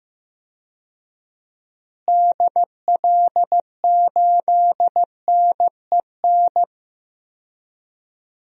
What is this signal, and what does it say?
Morse code, heard as a keyed tone, about 16 wpm, 700 Hz. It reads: DL8NEN
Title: DXCC-100 Call Signs - 6 Characters - Repeated 3x - 15wpm